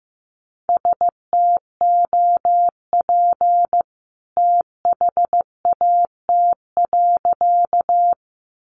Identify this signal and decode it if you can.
Morse code, heard as a keyed tone, about 15 wpm, 700 Hz. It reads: STOP THAT.
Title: A Sound Of Thunder 15wpm